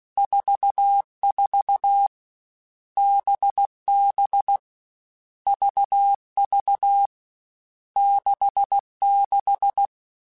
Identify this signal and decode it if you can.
Morse code, heard as a keyed tone, about 16 wpm, 800 Hz. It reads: 44 BB VV 66
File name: Groups@13 / Groups@13 mp3s / VB46@13